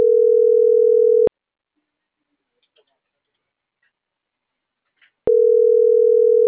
Ringback tone (North America), 440 + 480 Hz.